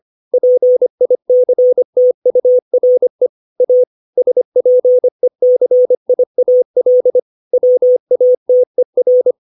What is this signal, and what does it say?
Morse code, heard as a keyed tone, about 25 wpm, 500 Hz. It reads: PICTURE A SPECIAL WATER
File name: 632 words / 632 Words 25WPM